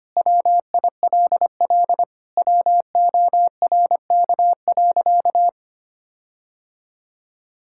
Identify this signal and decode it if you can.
Morse code, heard as a keyed tone, about 25 wpm, 700 Hz. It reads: WILL WORK.
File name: practice / 25wpm / Sentences with 4-Word Phrases - 25wpm